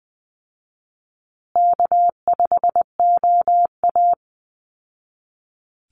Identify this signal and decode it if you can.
Morse code, heard as a keyed tone, about 20 wpm, 700 Hz. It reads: K5OA
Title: Call Signs Repeated 2x - 20wpm